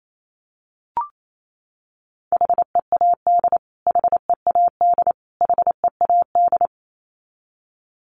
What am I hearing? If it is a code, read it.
Morse code, heard as a keyed tone, about 28 wpm, 700 Hz. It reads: HEAD HEAD HEAD
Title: Top 300 Words Repeated 28wpm